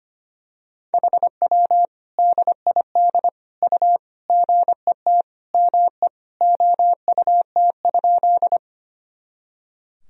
Morse code, heard as a keyed tone, about 25 wpm, 700 Hz. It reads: HW DID U GET ME OUT?
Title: Sentences from Top 200 Words and Abbreviations 25wpm